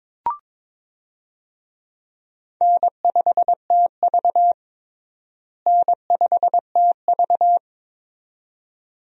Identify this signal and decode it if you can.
Morse code, heard as a keyed tone, about 22 wpm, 700 Hz. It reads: N5TV N5TV
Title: USA Call Signs - 4 Characters - Repeated 2x - 22wpm